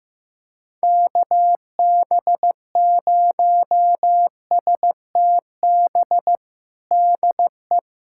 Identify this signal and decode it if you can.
Morse code, heard as a keyed tone, about 15 wpm, 700 Hz. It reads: KB0STB DE
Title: USA Callsign DE Callsign - 6 Characters - 15wpm